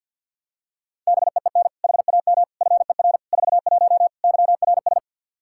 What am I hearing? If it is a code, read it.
Morse code, heard as a keyed tone, about 48 wpm, 700 Hz. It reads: BEEN HAM FER 41 YRS
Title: Practice QSO - Exchange 3 - More Info - 50wpm